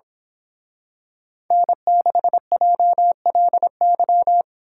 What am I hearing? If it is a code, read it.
Morse code, heard as a keyed tone, about 26 wpm, 700 Hz. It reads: N6JLY
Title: Speed Racing - USA Call Signs 22wpm